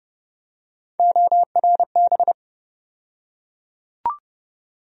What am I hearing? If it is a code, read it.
Morse code, heard as a keyed tone, about 29 wpm, 700 Hz. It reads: ORB
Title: USA Call Sign Postfixes - 3 Letters 30wpm